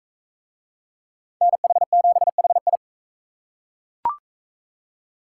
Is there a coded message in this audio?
Yes: Morse code from 1.4 to 2.8 s, heard as a keyed tone, about 40 wpm, 700 Hz. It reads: NH7HI